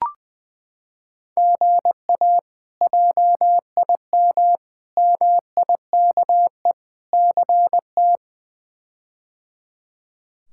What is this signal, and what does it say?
Morse code, heard as a keyed tone, about 20 wpm, 700 Hz. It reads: GA JIM MIKE CT
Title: K1USN SST QSO Practice - 20wpm